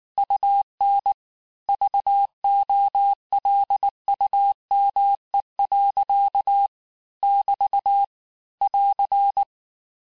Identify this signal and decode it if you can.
Morse code, heard as a keyed tone, about 19 wpm, 800 Hz. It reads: UN VOLUME. <BT> <AR>
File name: cours-du-lundi files / cours-1583--texte-clair-18-puis-23-mpm